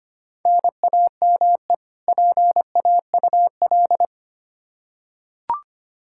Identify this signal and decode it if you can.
Morse code, heard as a keyed tone, about 25 wpm, 700 Hz. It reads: NAME PAUL
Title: Practice QSO - Exchange 1 - Beginning - 25wpm